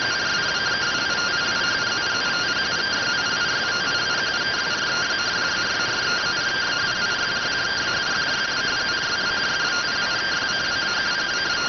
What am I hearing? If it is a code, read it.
Two-tone teleprinter signal, about 170 Hz shift.